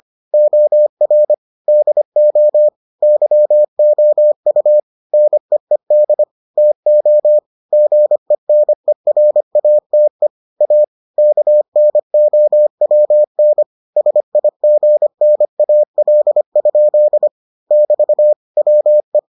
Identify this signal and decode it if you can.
Morse code, heard as a keyed tone, about 25 wpm, 600 Hz. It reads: OR DO YOU NEED TO GENERATE A KNOWN SIGNAL? <BT> WE